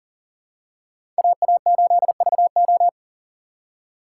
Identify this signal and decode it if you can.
Morse code, heard as a keyed tone, about 37 wpm, 700 Hz. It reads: AA8VO
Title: USA Call Signs - 5 Characters - 6x Character Spacing - 40wpm